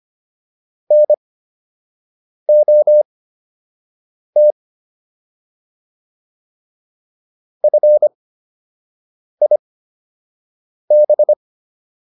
Morse code, heard as a keyed tone, about 25 wpm, spaced as for 6 wpm, 600 Hz. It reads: NOT FIB